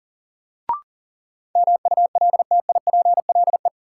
Morse code, heard as a keyed tone, about 40 wpm, 700 Hz. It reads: MULTIPLE
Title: Sets of 5 Words - New Word Drill - 40wpm